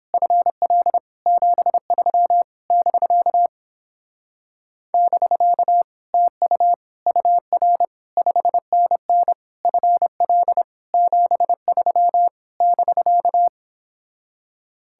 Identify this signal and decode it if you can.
Morse code, heard as a keyed tone, about 26 wpm, 700 Hz. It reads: FL 73 <BK> <BK> TU UR 5NN FL 73 <BK>
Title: Speed-Racing Common QSO Phrases - 22wpm